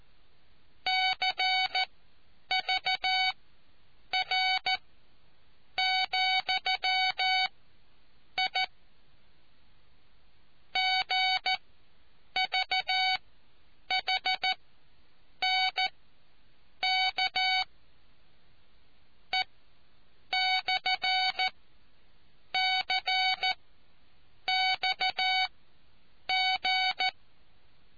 Morse code, heard as a keyed tone, about 14 wpm, spaced as for 7 wpm, 750 Hz. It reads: CVR,I GVHNK E/CXG